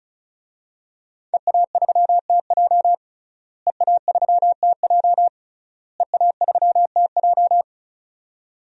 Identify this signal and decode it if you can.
Morse code, heard as a keyed tone, about 34 wpm, 700 Hz. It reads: EA3TJ EA3TJ EA3TJ
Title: DXCC-100 Call Signs - 5 Characters - Repeated 3x - 35wpm